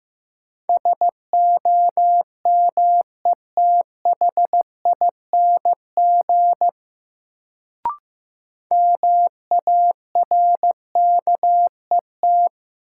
Morse code, heard as a keyed tone, about 15 wpm, 700 Hz. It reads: SOMETHING MARKET
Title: Sentences from Top 600 Words - New Word Drill - 15wpm